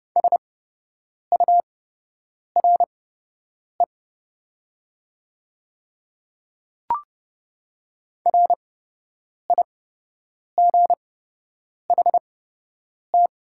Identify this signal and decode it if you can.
Morse code, heard as a keyed tone, about 30 wpm, spaced as for 8 wpm, 700 Hz. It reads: SURE RIGHT
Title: 8x Character Spacing - Top 300 Words - 30wpm